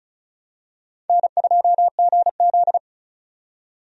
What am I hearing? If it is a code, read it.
Morse code, heard as a keyed tone, about 34 wpm, 700 Hz. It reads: N2GZ
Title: USA Call Signs - 4 Characters - 35wpm